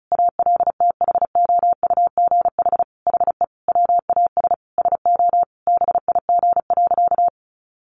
Morse code, heard as a keyed tone, about 34 wpm, 700 Hz. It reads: ALTHOUGH HE WAS SO BIG.
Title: Sentences from Top 600 Words 35wpm